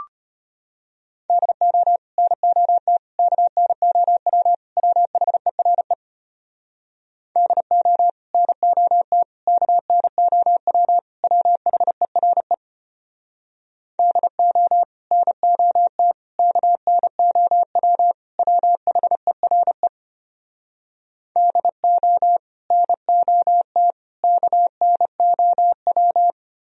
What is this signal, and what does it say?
Morse code, heard as a keyed tone, about 33 wpm, 700 Hz. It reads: DO NOT KNOW WHERE DO NOT KNOW WHERE DO NOT KNOW WHERE DO NOT KNOW